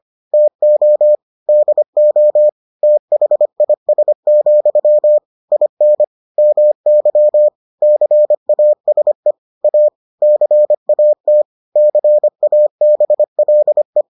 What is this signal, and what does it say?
Morse code, heard as a keyed tone, about 25 wpm, 600 Hz. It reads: TO DO THIS, IN MY CASE A CAT CABLE